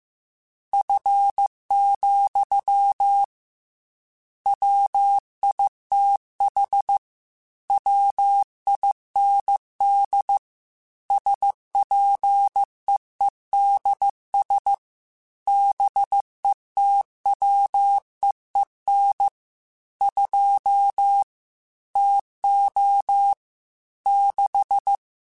Morse code, heard as a keyed tone, about 15 wpm, 800 Hz. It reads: F, WITH WIND SPEEDS BETWEEN 2 TO 6
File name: Weather Sets / Weather Sets-mp3s / Weather 4 10 @13